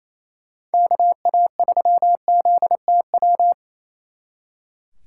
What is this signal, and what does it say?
Morse code, heard as a keyed tone, about 34 wpm, 700 Hz. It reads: KA3ZTW
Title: Speed Racing - USA 2x3 Call Signs 28wpm